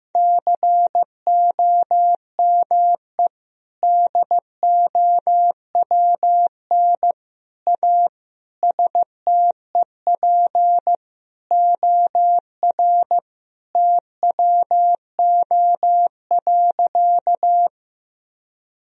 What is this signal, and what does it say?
Morse code, heard as a keyed tone, about 15 wpm, 700 Hz. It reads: COME DOWN A STEP OR TWO.
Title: Sentences from Top 1000 Words - Review Part 2 - 15wpm